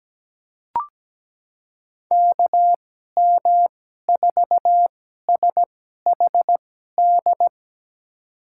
Morse code, heard as a keyed tone, about 17 wpm, spaced as for 12 wpm, 700 Hz. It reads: KM4SHD
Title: USA Call Signs - 6 Characters - 2x Character Spacing - 17wpm